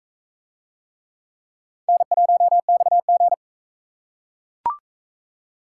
Morse code, heard as a keyed tone, about 40 wpm, 700 Hz. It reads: N1XG